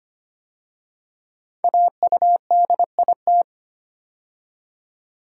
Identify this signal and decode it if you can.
Morse code, heard as a keyed tone, about 25 wpm, 700 Hz. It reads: AUDIT
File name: 5 Letter Words - First 3 Letters Given - 25wpm